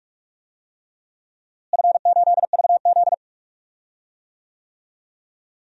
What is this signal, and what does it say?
Morse code, heard as a keyed tone, about 44 wpm, 700 Hz. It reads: F8VZ
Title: DXCC-100 Call Signs - 4 Characters - Repeated 3x - 45wpm